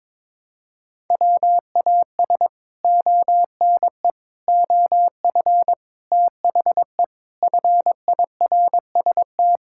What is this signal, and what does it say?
Morse code, heard as a keyed tone, about 22 wpm, 700 Hz. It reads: WAS ONE OF THE FIRST
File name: practice / 22wpm / Sets of 6 Words - 22wpm